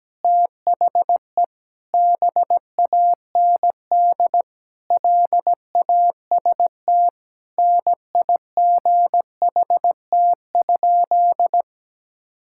Morse code, heard as a keyed tone, about 17 wpm, 700 Hz. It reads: THE BAND LAST NIGHT?
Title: Sentences from Top 1600 Words 17wpm